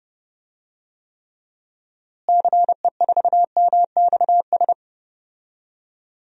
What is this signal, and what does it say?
Morse code, heard as a keyed tone, about 31 wpm, 700 Hz. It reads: CE4MXS
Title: DXCC-100 Call Signs - 6 Characters - Repeated 3x - 30wpm